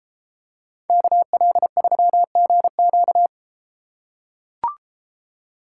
Morse code, heard as a keyed tone, about 33 wpm, 700 Hz. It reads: KL3GQ